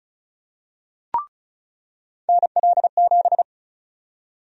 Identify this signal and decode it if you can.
Morse code, heard as a keyed tone, about 34 wpm, 700 Hz. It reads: NL7
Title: USA Call Sign Prefixes - 35wpm